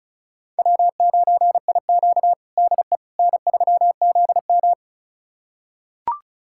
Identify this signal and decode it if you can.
Morse code, heard as a keyed tone, about 36 wpm, 700 Hz. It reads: W9IQ DE N3ZM
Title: USA Callsign DE Callsign - 4 Characters - 35wpm